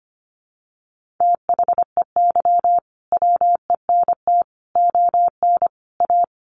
Morse code, heard as a keyed tone, about 25 wpm, 700 Hz. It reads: THEY WENT ON A
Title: Sentences from Top 1400 Words 25wpm